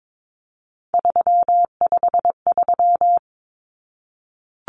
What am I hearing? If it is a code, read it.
Morse code, heard as a keyed tone, about 22 wpm, 700 Hz. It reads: 353
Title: Numbers - 3-Digit 22wpm